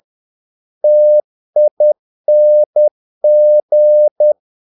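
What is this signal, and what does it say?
Morse code, heard as a keyed tone, about 10 wpm, 600 Hz. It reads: TING
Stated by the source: Morse practice of short Danish words at 10 wpm, from CW med Gnister.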